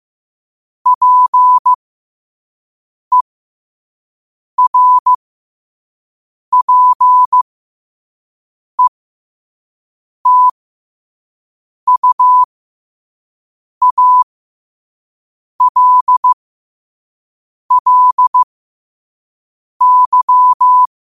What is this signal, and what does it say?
Morse code, heard as a keyed tone, about 15 wpm, spaced as for 5 wpm, 1000 Hz. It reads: PERPETUALLY
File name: qotdfiles / 5wpm